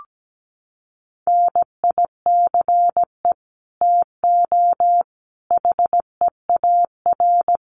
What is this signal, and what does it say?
Morse code, heard as a keyed tone, about 17 wpm, 700 Hz. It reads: NICE TO HEAR
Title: Common QSO Phrases 17wpm